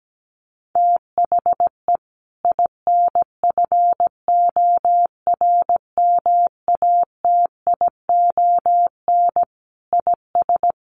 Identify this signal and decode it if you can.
Morse code, heard as a keyed tone, about 17 wpm, 700 Hz. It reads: THE INFORMATION IS
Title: Sentences from Top 1100 Words 17wpm